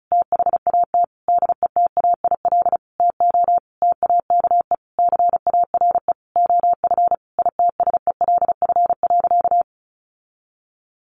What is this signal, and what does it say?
Morse code, heard as a keyed tone, about 34 wpm, 700 Hz. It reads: THAT DETAIL TO TAKE CARE OF ITSELF.
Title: Sentences from Top 1500 Words 35wpm